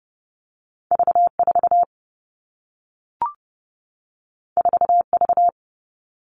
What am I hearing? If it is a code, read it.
Morse code, heard as a keyed tone, about 30 wpm, 700 Hz. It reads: V 4 4 V
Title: Two Challenging Letter-Number Combinations 30wpm